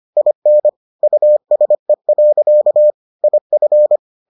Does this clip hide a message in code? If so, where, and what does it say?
Yes: Morse code from 0.2 to 4.0 s, heard as a keyed tone, about 25 wpm, 600 Hz. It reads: IN USE. IF